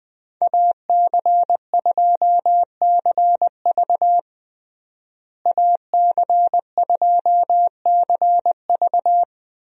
Morse code, heard as a keyed tone, about 20 wpm, 700 Hz. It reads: AC2CV AC2CV
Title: USA Call Signs - 5 Characters - Repeated 3x - 20wpm